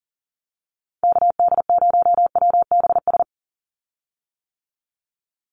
Morse code, heard as a keyed tone, about 40 wpm, 700 Hz. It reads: KD0WBS
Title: USA Call Signs 40wpm